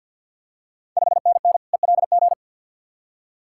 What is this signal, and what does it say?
Morse code, heard as a keyed tone, about 52 wpm, 690 Hz. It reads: 5NN ELG
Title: Ontario QSO Party Exchange 50wpm